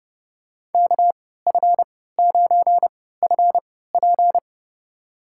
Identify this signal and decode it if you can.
Morse code, heard as a keyed tone, about 30 wpm, spaced as for 17 wpm, 700 Hz. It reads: KF9FP
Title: USA Call Signs - 5 Characters - 3x Character Spacing - 30wpm